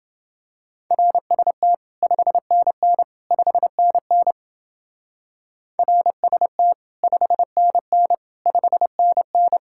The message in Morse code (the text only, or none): RST 5NN 5NN RST 5NN 5NN